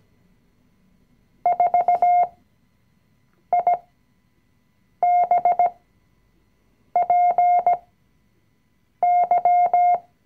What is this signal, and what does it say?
Morse code, heard as a keyed tone, about 17 wpm, spaced as for 6 wpm, 710 Hz. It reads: 4IBPY
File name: CW-Online-Kurs / Anfänger / Lektion 020